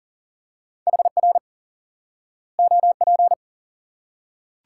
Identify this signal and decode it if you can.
Morse code, heard as a keyed tone, about 40 wpm, 700 Hz. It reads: HR OP
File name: IQE Course 3 - Lesson 89 - YAGI - Two Elements - 40wpm